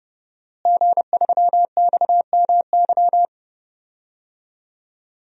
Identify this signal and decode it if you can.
Morse code, heard as a keyed tone, about 32 wpm, 700 Hz. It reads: G3XMY
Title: DXCC-100 Call Signs - 5 Characters - Repeated 2x - 30wpm